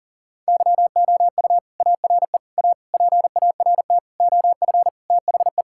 Morse code, heard as a keyed tone, about 40 wpm, 700 Hz. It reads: YOU ARE A PART OF THE
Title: Sentences from Top 1000 Words - Review Part 6 - 40wpm